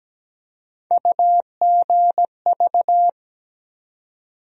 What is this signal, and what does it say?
Morse code, heard as a keyed tone, about 17 wpm, 700 Hz. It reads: UGV
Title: USA Call Sign Postfixes - 3 Letters 17wpm